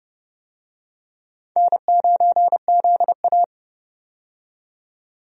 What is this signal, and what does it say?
Morse code, heard as a keyed tone, about 30 wpm, 700 Hz. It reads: N9ZA
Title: USA Call Signs - 4 Characters - 30wpm